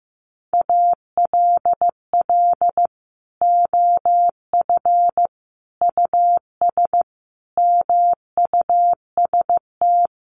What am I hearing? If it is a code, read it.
Morse code, heard as a keyed tone, about 15 wpm, 700 Hz. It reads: ALL OF US MUST
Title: Sentences from Top 1400 Words 15wpm